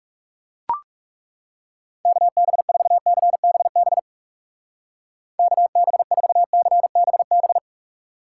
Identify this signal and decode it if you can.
Morse code, heard as a keyed tone, about 37 wpm, 700 Hz. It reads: KB4CBB KB4CBB
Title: Speed Racing - USA 2x3 Call Signs 30wpm